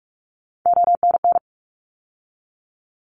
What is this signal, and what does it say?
Morse code, heard as a keyed tone, about 44 wpm, 700 Hz. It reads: ONN
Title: ARRL Sections 45wpm